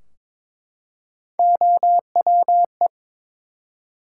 Morse code, heard as a keyed tone, about 22 wpm, 700 Hz. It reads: OWE
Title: Three Letter Words - 22wpm